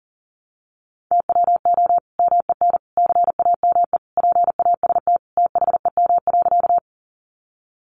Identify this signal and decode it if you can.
Morse code, heard as a keyed tone, about 40 wpm, 700 Hz. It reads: TWO MEN CAME PAST THEM.
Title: Sentences from Top 600 Words 40wpm